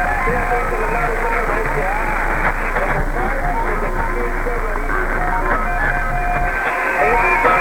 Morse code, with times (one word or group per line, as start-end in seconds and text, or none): none